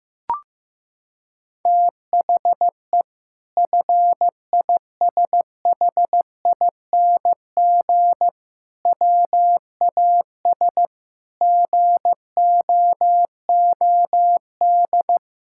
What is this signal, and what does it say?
Morse code, heard as a keyed tone, about 15 wpm, 700 Hz. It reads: THE FISHING WAS GOOD
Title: Sentences from Top 1000 Words - Review Part 5 - 15wpm